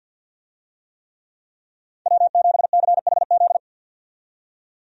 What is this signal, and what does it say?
Morse code, heard as a keyed tone, about 47 wpm, 700 Hz. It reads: W7CHZ
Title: USA Call Signs - 5 Characters - 50wpm